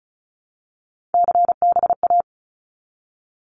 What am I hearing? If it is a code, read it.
Morse code, heard as a keyed tone, about 36 wpm, 700 Hz. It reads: CBA
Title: Abbreviations 35wpm